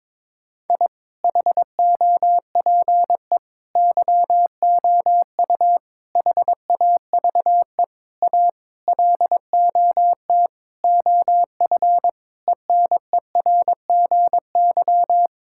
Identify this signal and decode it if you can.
Morse code, heard as a keyed tone, about 22 wpm, 700 Hz. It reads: I HOPE YOU HAVE A LOT OF ENERGY
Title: Sentences from Top 1000 Words - Review Part 6 - 22wpm